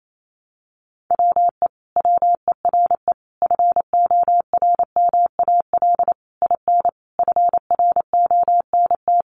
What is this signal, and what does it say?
Morse code, heard as a keyed tone, about 28 wpm, 700 Hz. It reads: WE WERE FORMAL IN FRONT
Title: Sentences from Top 1300 Words 28wpm